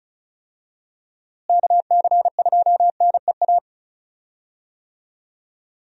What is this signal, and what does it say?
Morse code, heard as a keyed tone, about 34 wpm, 700 Hz. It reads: KC2NEA
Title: USA Call Signs - 6 Characters - Repeated 2x - 35wpm